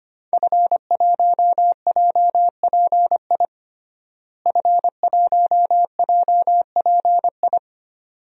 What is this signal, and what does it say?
Morse code, heard as a keyed tone, about 25 wpm, 700 Hz. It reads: F1JPI F1JPI
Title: DXCC-100 Call Signs - 5 Characters - Repeated 2x - 25wpm